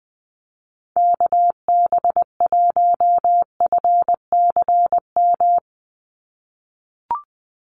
Morse code, heard as a keyed tone, about 20 wpm, 700 Hz. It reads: KB1FCM
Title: USA Call Signs 20wpm